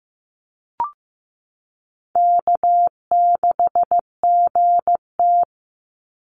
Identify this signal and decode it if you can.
Morse code, heard as a keyed tone, about 15 wpm, 700 Hz. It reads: K6GT
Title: Easy Call Signs Repeated 3x - 15wpm